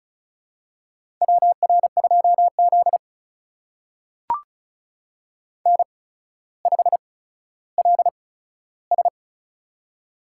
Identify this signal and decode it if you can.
Morse code, heard as a keyed tone, about 34 wpm, spaced as for 15 wpm, 700 Hz. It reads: WR2Z N5LS
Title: USA Call Signs - 4 Characters - 8x Character Spacing - 35wpm